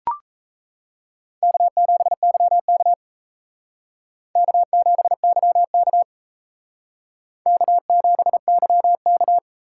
Morse code, heard as a keyed tone, about 35 wpm, 700 Hz. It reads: K7YK K7YK K7YK
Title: Speed Racing - USA 1x2 Call Signs 28wpm